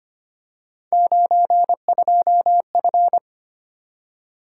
Morse code, heard as a keyed tone, about 25 wpm, 700 Hz. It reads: 92F
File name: Three Letter-Number Combinations - 25wpm